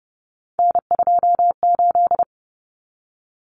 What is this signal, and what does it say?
Morse code, heard as a keyed tone, about 30 wpm, 700 Hz. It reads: N28
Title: Common Cut Numbers - 3-Digit 30wpm